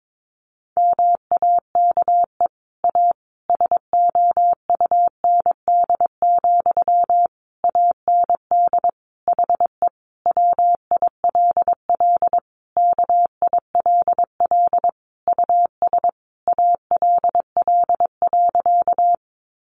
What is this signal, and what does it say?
Morse code, heard as a keyed tone, about 22 wpm, 700 Hz. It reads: MAKE A SOUND, AND HE WILL KILL US ALL.
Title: Sentences from Top 1400 Words 22wpm